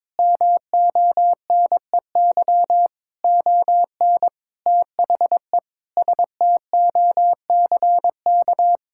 Morse code, heard as a keyed tone, about 22 wpm, 700 Hz. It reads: MONEY ON THE STOCK
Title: Sentences from Top 1000 Words - Review Part 6 - 22wpm